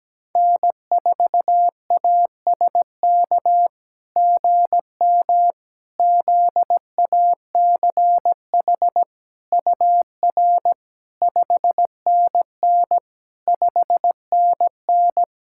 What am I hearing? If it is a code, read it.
Morse code, heard as a keyed tone, about 17 wpm, 700 Hz. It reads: N4ASK GM ZACH UR 5NN 5NN
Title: SOTA QSO Practice 17wpm